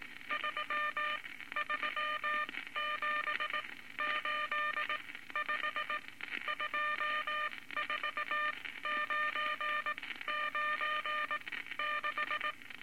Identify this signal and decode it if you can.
Morse code, heard as a keyed tone, about 19 wpm, spaced as for 14 wpm, 1200 Hz, with noise. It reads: 33785 24996
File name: Morse Code Records / 19 A LCM 166 Ten Words Per Minute